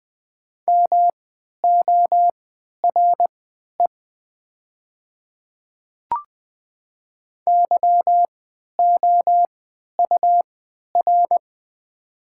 Morse code, heard as a keyed tone, about 20 wpm, spaced as for 11 wpm, 700 Hz. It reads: MORE YOUR
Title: 3x Character Spacing - Top 100 Words - 20wpm